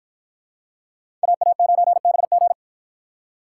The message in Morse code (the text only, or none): AA9BG